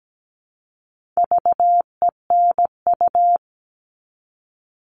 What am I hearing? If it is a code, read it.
Morse code, heard as a keyed tone, about 17 wpm, 700 Hz. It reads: VENU
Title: All Cut Numbers - 4-Digit 17wpm